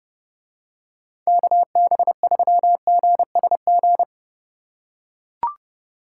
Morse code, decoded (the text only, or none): KB3GSG